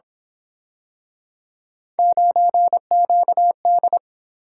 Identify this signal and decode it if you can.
Morse code, heard as a keyed tone, about 28 wpm, 700 Hz. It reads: 9QD
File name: Speed-Racing - Three Letter-Number Combinations - 22wpm